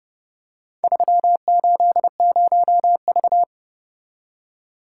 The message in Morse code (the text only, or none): 380V